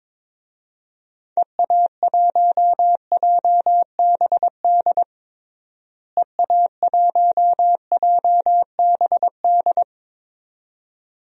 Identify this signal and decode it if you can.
Morse code, heard as a keyed tone, about 22 wpm, 700 Hz. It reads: EA1JBD EA1JBD
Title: DXCC-100 Call Signs - 6 Characters - Repeated 2x - 22wpm